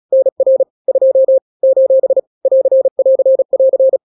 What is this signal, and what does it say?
Morse code, heard as a keyed tone, about 36 wpm, 530 Hz. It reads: NR 2 : <AR><AR><AR>
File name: daily / 35wpm